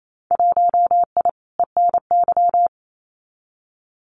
Morse code, heard as a keyed tone, about 28 wpm, 700 Hz. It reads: 1I ENY
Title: Winter Field Day Exchanges Encore 28wpm